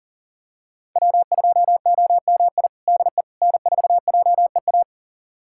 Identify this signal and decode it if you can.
Morse code, heard as a keyed tone, about 37 wpm, 700 Hz. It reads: W2OMI DE N4JEA
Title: USA Callsign DE Callsign - 5 Characters - 40wpm